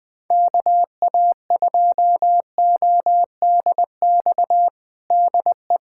Morse code, heard as a keyed tone, about 19 wpm, 700 Hz. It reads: KA2ODX DE
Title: USA Callsign DE Callsign - 6 Characters - 20wpm